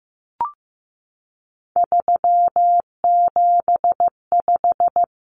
Morse code, heard as a keyed tone, about 15 wpm, 700 Hz. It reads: 375
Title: Common Cut Numbers - 3-Digit 15wpm